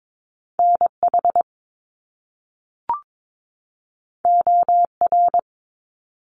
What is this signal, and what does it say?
Morse code, heard as a keyed tone, about 22 wpm, 700 Hz. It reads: NH OR
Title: US States 22wpm